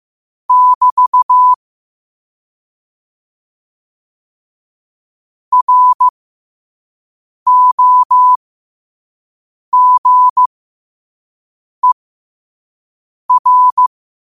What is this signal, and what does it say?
Morse code, heard as a keyed tone, about 15 wpm, spaced as for 5 wpm, 1000 Hz. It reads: <BT> ROGER
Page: New quotes every day in morse code at 5 Words per minute.